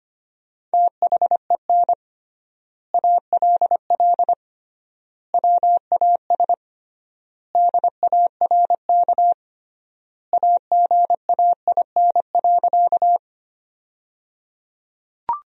Morse code, heard as a keyed tone, about 25 wpm, 700 Hz. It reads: THEN ALL WAS DARK AGAIN.
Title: 3x Word-Spacing - Sentences from Top 500 Words - 25wpm